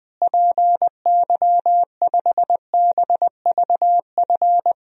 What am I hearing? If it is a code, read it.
Morse code, heard as a keyed tone, about 20 wpm, 700 Hz. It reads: PY5BVF
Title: DXCC-100 Call Signs - 6 Characters - Repeated 3x - 20wpm